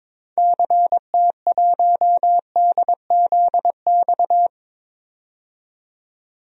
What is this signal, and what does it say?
Morse code, heard as a keyed tone, about 22 wpm, 700 Hz. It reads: CT1DZX
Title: DXCC-100 Call Signs - 6 Characters - 22wpm